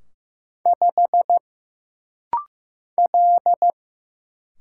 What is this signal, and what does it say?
Morse code, heard as a keyed tone, about 15 wpm, 700 Hz. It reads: 5 L
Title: Single Letter-Number Alt - Rapid-Fire 15wpm